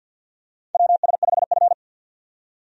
Morse code, heard as a keyed tone, about 51 wpm, 700 Hz. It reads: WS5F
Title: USA Call Signs - 4 Characters - 50wpm